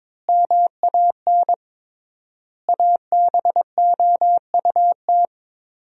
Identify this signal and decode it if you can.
Morse code, heard as a keyed tone, about 22 wpm, 700 Hz. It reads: MAN ABOUT
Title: 3x Word-Spacing - Sentences from Top 400 Words - 22wpm